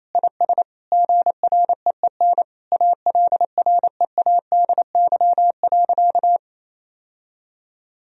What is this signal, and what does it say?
Morse code, heard as a keyed tone, about 28 wpm, 700 Hz. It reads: IS GREEN ALREADY.
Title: Sentences from Top 1000 Words - Review Part 2 - 28wpm